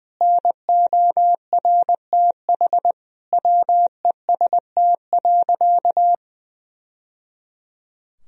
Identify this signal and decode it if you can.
Morse code, heard as a keyed tone, about 20 wpm, 700 Hz. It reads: NORTH WEST.
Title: Sentences from Top 1000 Words - Review Part 2 - 20wpm